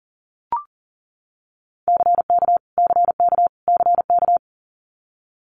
Morse code, heard as a keyed tone, about 37 wpm, 690 Hz. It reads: CK CK CK
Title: Multi-letter Phonemes Repeated 3x 40wpm